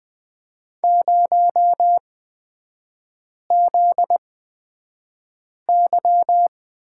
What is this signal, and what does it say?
Morse code, heard as a keyed tone, about 20 wpm, spaced as for 5 wpm, 700 Hz. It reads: 0ZY
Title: 243 Lesson 39e - Letter z - Three Characters 20wpm